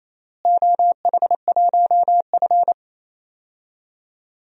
Morse code, heard as a keyed tone, about 28 wpm, 700 Hz. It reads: OH1F
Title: DXCC-100 Call Signs - 4 Characters - Repeated 3x - 28wpm